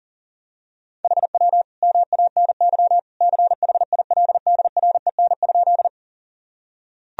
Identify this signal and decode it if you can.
Morse code, heard as a keyed tone, about 37 wpm, 700 Hz. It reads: HW MANY CHILDREN?
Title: Sentences from Top 300 Words and Abbreviations 40wpm